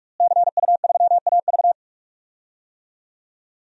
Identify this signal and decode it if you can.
Morse code, heard as a keyed tone, about 47 wpm, 700 Hz. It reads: CU3AV
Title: DXCC-100 Call Signs - 5 Characters - Repeated 2x - 45wpm